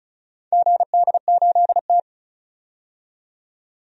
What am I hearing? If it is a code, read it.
Morse code, heard as a keyed tone, about 36 wpm, 700 Hz. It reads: GD8T
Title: DXCC-100 Call Signs - 4 Characters - 35wpm